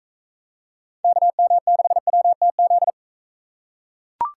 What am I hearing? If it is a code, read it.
Morse code, heard as a keyed tone, about 40 wpm, 700 Hz. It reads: KM6WTZ